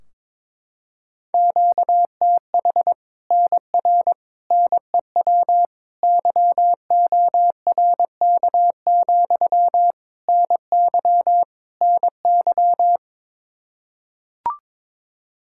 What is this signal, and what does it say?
Morse code, heard as a keyed tone, about 22 wpm, 700 Hz. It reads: QTH NR NEW YORK, NY NY
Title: Practice QSO - Exchange 1 - Beginning - 22wpm